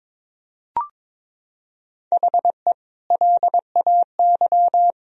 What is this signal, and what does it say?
Morse code, heard as a keyed tone, about 22 wpm, 700 Hz. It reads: HE LAY
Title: Sentences from Top 1000 Words - Review Part 2 - 22wpm